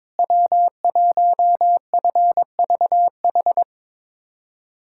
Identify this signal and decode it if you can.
Morse code, heard as a keyed tone, about 22 wpm, 700 Hz. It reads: W1FVH
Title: USA Callsign DE Callsign - 5 Characters - 22wpm